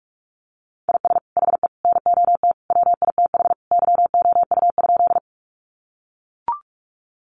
Morse code, heard as a keyed tone, about 44 wpm, 700 Hz. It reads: IS HE NOT WITH YOU?
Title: Sentences from Top 600 Words 45wpm